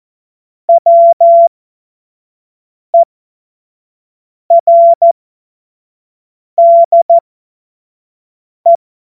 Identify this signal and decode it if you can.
Morse code, heard as a keyed tone, about 14 wpm, spaced as for 5 wpm, 680 Hz. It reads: WERDE
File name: Morsekurs / Witze / Witz 01. August 2023 25 F70